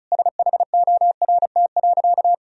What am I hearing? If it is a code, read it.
Morse code, heard as a keyed tone, about 36 wpm, 700 Hz. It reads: SHORT.
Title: Sentences from Top 500 Words and Abbreviations 35wpm